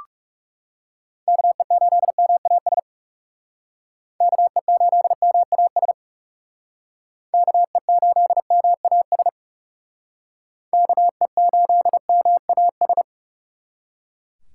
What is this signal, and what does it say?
Morse code, heard as a keyed tone, about 38 wpm, 700 Hz. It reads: KE8MAS KE8MAS KE8MAS KE8MAS